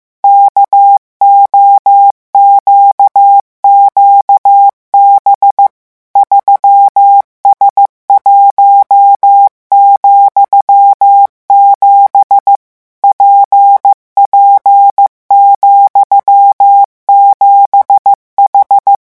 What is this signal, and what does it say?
Morse code, heard as a keyed tone, about 15 wpm, 800 Hz. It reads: KOQQB 3S1,7 PP,7H
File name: morse mp3 / revision13-vitesse 14 mots